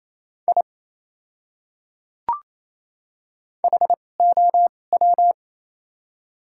Morse code, heard as a keyed tone, about 28 wpm, spaced as for 20 wpm, 700 Hz. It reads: I HOW
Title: 2x Character Spacing - Top 100 Words - 28wpm